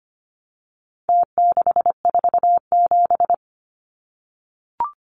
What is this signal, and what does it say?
Morse code, heard as a keyed tone, about 25 wpm, 700 Hz. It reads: T647
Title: Numbers - Contest Style - 4-Digit 25wpm